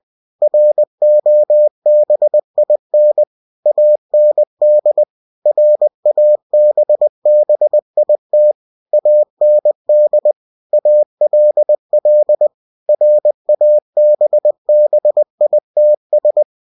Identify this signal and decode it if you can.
Morse code, heard as a keyed tone, about 20 wpm, 600 Hz. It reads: ROBIN AND RABBIT AND ALL RABBITS
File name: cw-books / winnie-the-pooh / winnie-20wpm-page-0020